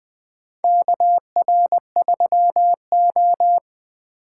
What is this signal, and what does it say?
Morse code, heard as a keyed tone, about 20 wpm, 700 Hz. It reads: KR3O
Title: USA Call Signs - 4 Characters - 6x Character Spacing - 20wpm